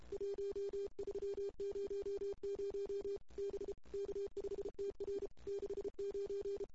Morse code, heard as a keyed tone, about 31 wpm, 400 Hz. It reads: 1300 B K5TR 69
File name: CW / 2008-ss-cw-k5tr-op-n5rz -19:00